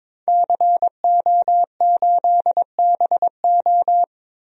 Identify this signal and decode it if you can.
Morse code, heard as a keyed tone, about 22 wpm, 700 Hz. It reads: CO8BO
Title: DXCC-100 Call Signs - 5 Characters - 22wpm